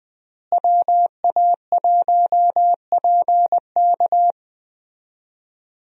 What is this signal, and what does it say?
Morse code, heard as a keyed tone, about 20 wpm, 700 Hz. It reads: WA1PK
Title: USA Call Signs - 5 Characters - Repeated 2x - 20wpm